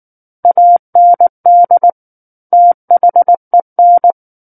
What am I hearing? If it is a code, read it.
Morse code, heard as a keyed tone, about 19 wpm, 700 Hz. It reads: AND THEN